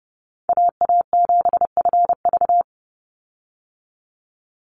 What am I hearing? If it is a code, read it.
Morse code, heard as a keyed tone, about 30 wpm, 700 Hz. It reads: AA7FV
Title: Call Signs - 30wpm